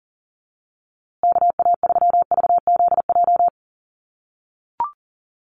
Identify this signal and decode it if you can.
Morse code, heard as a keyed tone, about 40 wpm, 700 Hz. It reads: KA3VZJ
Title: USA Call Signs Repeated 3x 40wpm